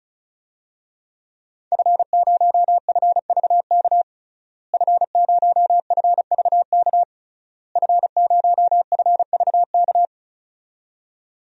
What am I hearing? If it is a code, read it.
Morse code, heard as a keyed tone, about 36 wpm, 700 Hz. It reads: F0FVK F0FVK F0FVK
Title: DXCC-100 Call Signs - 5 Characters - Repeated 3x - 35wpm